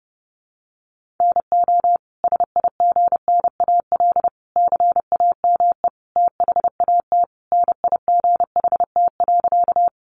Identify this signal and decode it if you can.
Morse code, heard as a keyed tone, about 30 wpm, 700 Hz. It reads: NO SIGNAL CAME THAT NIGHT.
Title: Sentences from Top 1400 Words 30wpm